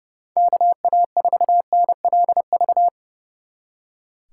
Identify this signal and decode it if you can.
Morse code, heard as a keyed tone, about 32 wpm, 700 Hz. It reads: KA4NLV
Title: Speed Racing - USA 2x3 Call Signs 30wpm